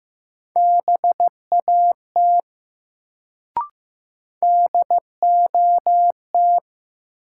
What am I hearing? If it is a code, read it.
Morse code, heard as a keyed tone, about 15 wpm, 700 Hz. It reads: BAT DOT
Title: Sentences from Top 1500 Words - New Word Drill - 15wpm